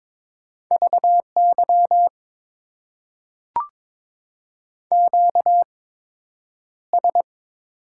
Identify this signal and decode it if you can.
Morse code, heard as a keyed tone, about 22 wpm, spaced as for 9 wpm, 700 Hz. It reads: VY QS
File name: Top 90 QSO Elements - 8x Character Spacing - 22wpm